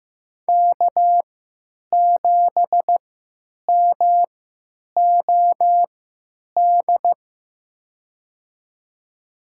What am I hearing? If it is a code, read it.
Morse code, heard as a keyed tone, about 15 wpm, spaced as for 9 wpm, 700 Hz. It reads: K7MOD
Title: USA Call Signs - 3x Character Spacing - 15wpm